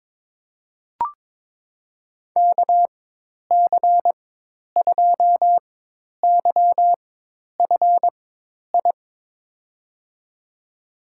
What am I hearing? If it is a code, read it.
Morse code, heard as a keyed tone, about 22 wpm, spaced as for 10 wpm, 700 Hz. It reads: KC2YFI
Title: USA Call Signs - 4x Character Spacing - 22wpm